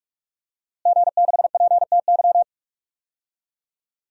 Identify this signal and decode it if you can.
Morse code, heard as a keyed tone, about 42 wpm, 700 Hz. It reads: G6PTY
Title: DXCC-100 Call Signs - 5 Characters - Repeated 2x - 45wpm